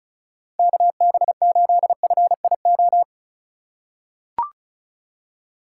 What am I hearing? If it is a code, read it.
Morse code, heard as a keyed tone, about 34 wpm, 700 Hz. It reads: KB8FIO
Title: USA Call Signs - 6 Characters - 3x Character Spacing - 35wpm